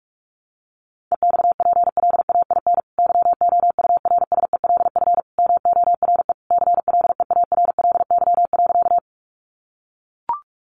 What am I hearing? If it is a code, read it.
Morse code, heard as a keyed tone, about 44 wpm, 700 Hz. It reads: EXPLAIN YOURSELF MORE CLEARLY.
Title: Sentences from Top 1300 Words 45wpm